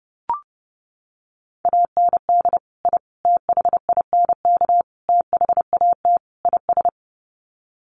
Morse code, heard as a keyed tone, about 28 wpm, 700 Hz. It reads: AND I THINK THAT IS
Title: Sets of 5 Words Encore2 30wpm